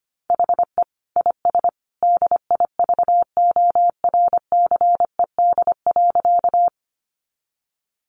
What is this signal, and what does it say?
Morse code, heard as a keyed tone, about 25 wpm, 700 Hz. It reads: HE IS DIVORCED.
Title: Additional Easy Phrases 25wpm